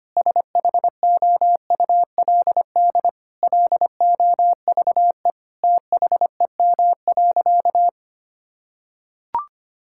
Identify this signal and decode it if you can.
Morse code, heard as a keyed tone, about 25 wpm, 700 Hz. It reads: SHOULD LOVE THEM.
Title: Sentences from Top 1600 Words 25wpm